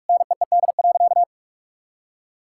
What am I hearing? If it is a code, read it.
Morse code, heard as a keyed tone, about 42 wpm, 700 Hz. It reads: NEED.
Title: Sentences from Top 1000 Words - Review Part 6 - 45wpm